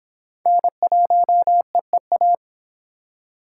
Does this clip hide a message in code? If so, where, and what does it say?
Yes: Morse code from 0.5 to 2.4 s, heard as a keyed tone, about 24 wpm, 700 Hz. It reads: N1EEA